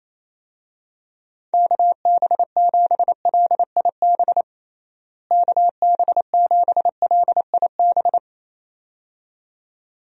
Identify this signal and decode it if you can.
Morse code, heard as a keyed tone, about 28 wpm, 700 Hz. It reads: KB7LIB KB7LIB
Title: USA Call Signs - 6 Characters - Repeated 2x - 28wpm